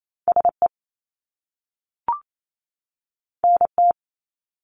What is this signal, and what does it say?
Morse code, heard as a keyed tone, about 28 wpm, 700 Hz. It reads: SE NT
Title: Top Two Letter Combinations in Words 28wpm